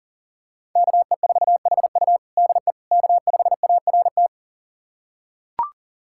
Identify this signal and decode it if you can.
Morse code, heard as a keyed tone, about 37 wpm, 700 Hz. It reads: KE4HU DE K5ART
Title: USA Callsign DE Callsign - 5 Characters - 40wpm